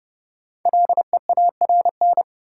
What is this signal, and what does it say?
Morse code, heard as a keyed tone, about 31 wpm, 700 Hz. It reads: LEARN
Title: Sentences from Top 300 Words - New Word Drill - 30wpm